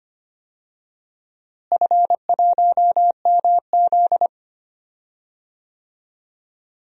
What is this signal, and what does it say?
Morse code, heard as a keyed tone, about 25 wpm, 700 Hz. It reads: F1MZ
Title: DXCC-100 Call Signs - 4 Characters - Repeated 3x - 25wpm